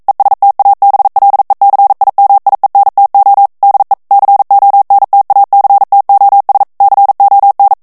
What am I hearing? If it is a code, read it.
Morse code, heard as a keyed tone, about 42 wpm, 800 Hz. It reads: ESTABLECIMIENTO DE CONTACTOS CON
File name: morse mp3 / ESPAGNOL TEXTE EN CLAIR 1 VITESSE 40 MOTS MINUTES